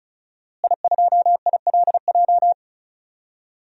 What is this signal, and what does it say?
Morse code, heard as a keyed tone, about 34 wpm, 700 Hz. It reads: I2ILJ
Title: DXCC-100 Call Signs - 5 Characters - Repeated 2x - 35wpm